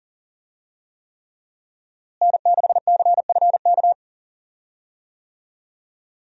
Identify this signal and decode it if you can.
Morse code, heard as a keyed tone, about 40 wpm, 700 Hz. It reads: N6CFK
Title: USA Call Signs - 5 Characters - Repeated 3x - 40wpm